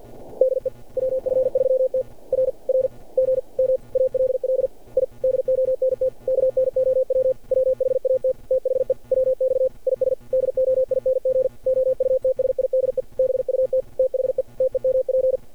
Morse code, heard as a keyed tone, about 49 wpm, 530 Hz. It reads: DE WF3T R R GM ALL I DONT KNOW WHAT THE WX IS DOING OUTSIDE BUT THE TEMP
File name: archived / CRSnet